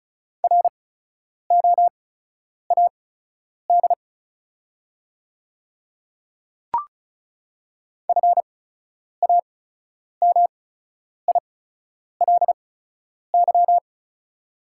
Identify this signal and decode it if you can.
Morse code, heard as a keyed tone, about 36 wpm, spaced as for 10 wpm, 700 Hz. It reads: ROAD FAMILY
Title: 8x Character Spacing - Top 500 Words - 35wpm